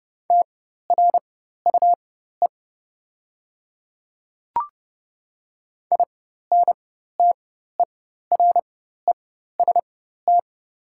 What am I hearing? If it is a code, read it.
Morse code, heard as a keyed tone, about 29 wpm, spaced as for 14 wpm, 700 Hz. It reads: TRUE INTEREST
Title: 4x Character Spacing - Top 500 Words - 30wpm